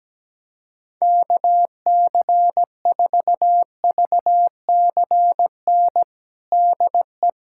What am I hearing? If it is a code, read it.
Morse code, heard as a keyed tone, about 17 wpm, 700 Hz. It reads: KC4VCN DE
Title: USA Callsign DE Callsign - 6 Characters - 17wpm